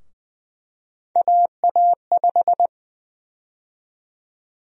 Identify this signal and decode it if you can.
Morse code, heard as a keyed tone, about 20 wpm, 700 Hz. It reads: AA5
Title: USA Call Sign Prefixes - 20wpm